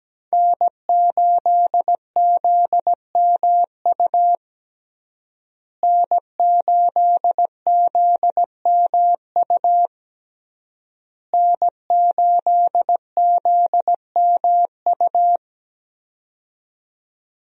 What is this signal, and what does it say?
Morse code, heard as a keyed tone, about 17 wpm, 700 Hz. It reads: N8ZMU N8ZMU N8ZMU
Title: USA Call Signs - 5 Characters - Repeated 3x - 17wpm